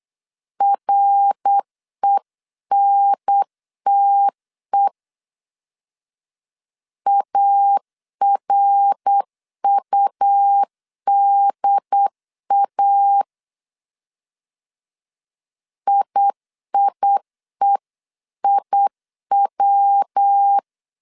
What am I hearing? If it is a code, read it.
Morse code, heard as a keyed tone, about 8 wpm, 800 Hz. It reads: RENTE ARUDA IIEIW